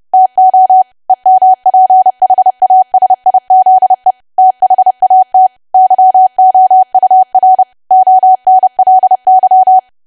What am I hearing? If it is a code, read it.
Morse code, heard as a keyed tone, about 30 wpm, 750 Hz. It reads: TO EMPHASIZE THAT YOUR ONLY
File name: cwa / W1AW-Level3 / 30WPM